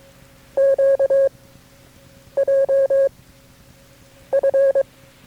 Morse code, heard as a keyed tone, about 22 wpm, 540 Hz. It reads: Q J F
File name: FreeFiles / CW-practice / KB6MT-tape3-side1